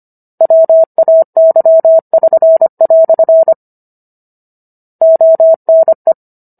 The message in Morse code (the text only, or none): WAY<SN>" ONE